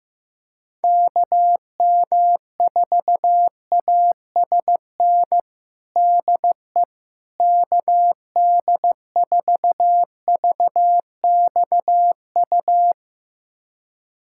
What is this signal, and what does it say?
Morse code, heard as a keyed tone, about 16 wpm, 700 Hz. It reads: KM4ASN DE KD4VXU
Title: USA Callsign DE Callsign - 6 Characters - 15wpm